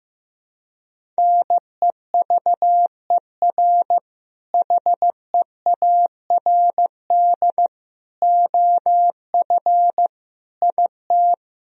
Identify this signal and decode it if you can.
Morse code, heard as a keyed tone, about 15 wpm, 700 Hz. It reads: NEVER HEARD OF IT
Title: Sentences from Top 400 Words and Abbreviations 15wpm